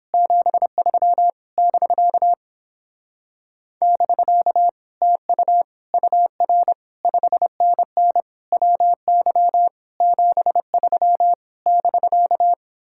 Morse code, heard as a keyed tone, about 29 wpm, 700 Hz. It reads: 73 <BK> <BK> TU UR 5NN WY 73 <BK>